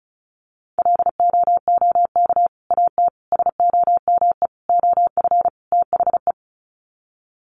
Morse code, heard as a keyed tone, about 34 wpm, 700 Hz. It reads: LOOK AT SOME OF THE
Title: Sets of 5 Words Encore 35wpm